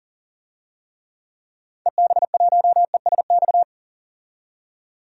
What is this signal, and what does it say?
Morse code, heard as a keyed tone, about 37 wpm, 700 Hz. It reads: EB1ESX
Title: DXCC-100 Call Signs - 6 Characters - 40wpm